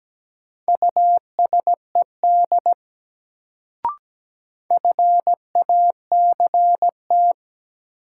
Morse code, heard as a keyed tone, about 17 wpm, 700 Hz. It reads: USED FACT
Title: Sets of 2 Words - New Word Drill - 17wpm